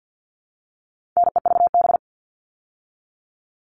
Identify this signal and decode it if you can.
Morse code, heard as a keyed tone, about 50 wpm, 700 Hz. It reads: NEVB
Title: All Cut Numbers - 4-Digit 50wpm